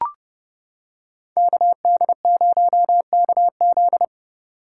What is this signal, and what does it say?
Morse code, heard as a keyed tone, about 29 wpm, 700 Hz. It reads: KD0KZ